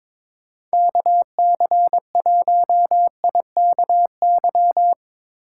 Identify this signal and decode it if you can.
Morse code, heard as a keyed tone, about 23 wpm, 700 Hz. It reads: KC1IKY
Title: USA Call Signs - 6 Characters - 2x Character Spacing - 22wpm